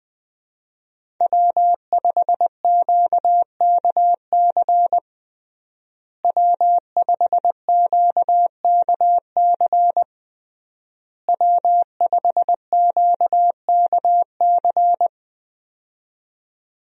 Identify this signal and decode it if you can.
Morse code, heard as a keyed tone, about 20 wpm, 700 Hz. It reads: W5QKC W5QKC W5QKC
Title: USA Call Signs - 5 Characters - Repeated 3x - 20wpm